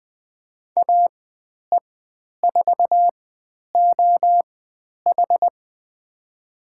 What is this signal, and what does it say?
Morse code, heard as a keyed tone, about 20 wpm, spaced as for 10 wpm, 700 Hz. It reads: AE4OH
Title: 095 Lesson 16g - Letter c - Call Signs 20wpm